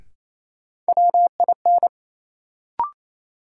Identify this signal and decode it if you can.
Morse code, heard as a keyed tone, about 28 wpm, 700 Hz. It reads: WIN
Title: Sentences from Top 1500 Words - New Word Drill - 28wpm